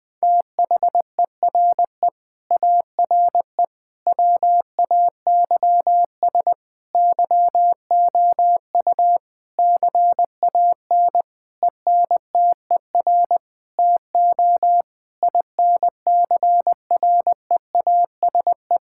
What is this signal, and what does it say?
Morse code, heard as a keyed tone, about 20 wpm, 700 Hz. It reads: THERE ARE WAYS YOU CAN ENTER TO INCREASE
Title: Sentences with 8-Word Phrases - Spoken Beginning - 20wpm